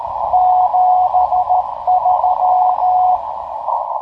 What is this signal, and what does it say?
Morse code, heard as a keyed tone, about 13 wpm, 730 Hz, with radio noise. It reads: 73
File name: tracks / hp1rcp